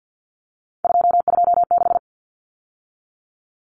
Morse code, heard as a keyed tone, about 48 wpm, 700 Hz. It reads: 226
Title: Numbers - 3-Digit 50wpm